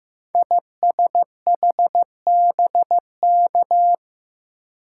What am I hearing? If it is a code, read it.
Morse code, heard as a keyed tone, about 15 wpm, 700 Hz. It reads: ISHBK
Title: Code Groups - 5 Letters - 15wpm